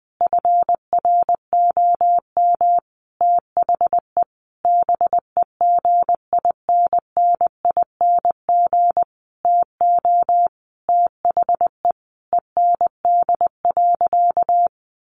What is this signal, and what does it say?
Morse code, heard as a keyed tone, about 20 wpm, 700 Hz. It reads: FROM THE BEGINNING TO THE END.
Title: Sentences from Top 1300 Words 20wpm